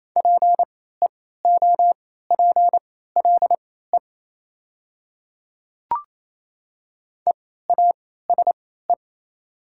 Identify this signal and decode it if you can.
Morse code, heard as a keyed tone, about 28 wpm, spaced as for 16 wpm, 700 Hz. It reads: PEOPLE EASE
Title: 3x Character Spacing - Top 300 Words - 28wpm